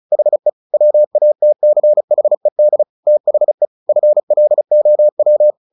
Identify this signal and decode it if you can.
Morse code, heard as a keyed tone, about 34 wpm, 600 Hz. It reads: HE WATCHED THE FLOW
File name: the old man and the sea 2 35wpm